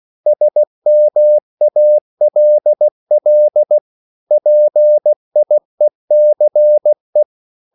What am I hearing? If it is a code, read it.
Morse code, heard as a keyed tone, about 16 wpm, 600 Hz. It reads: SMALL PIECE